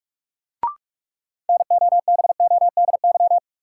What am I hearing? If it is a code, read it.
Morse code, heard as a keyed tone, about 46 wpm, 700 Hz. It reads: NOBODY
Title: Sentences from Top 1600 Words - New Word Drill Alt - 45wpm